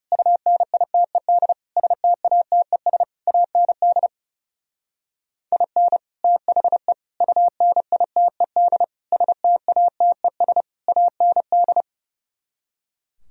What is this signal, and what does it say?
Morse code, heard as a keyed tone, about 34 wpm, 700 Hz. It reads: UNITED STATES AND IN THE UNITED STATES AND
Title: Speed-Racing - Sets of 5 Words 30wpm